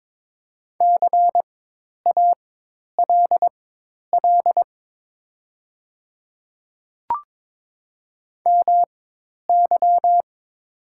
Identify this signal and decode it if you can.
Morse code, heard as a keyed tone, about 23 wpm, spaced as for 10 wpm, 700 Hz. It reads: CALL MY
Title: 4x Character Spacing - Top 100 Words - 22wpm